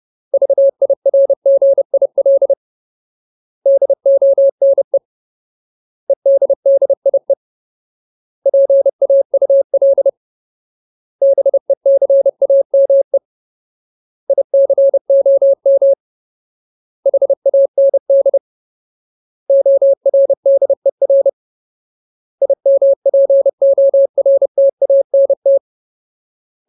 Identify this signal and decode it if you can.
Morse code, heard as a keyed tone, about 30 wpm, 550 Hz. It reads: VIRGIL DONE EDDIE PAUL BECAME ICOM HAND ORDER IMPORTANT
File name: Ultimate Word List / 30WPM Ultimate